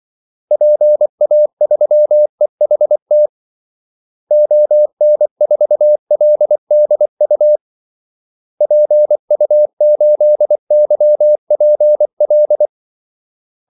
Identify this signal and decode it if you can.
Morse code, heard as a keyed tone, about 24 wpm, 600 Hz. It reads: PA3EHT ON4LDU PU8YPL
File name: Callsigns-24wpm